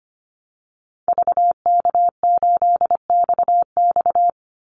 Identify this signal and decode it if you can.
Morse code, heard as a keyed tone, about 25 wpm, 700 Hz. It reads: VK8XX
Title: Call Signs Repeated 2x - 25wpm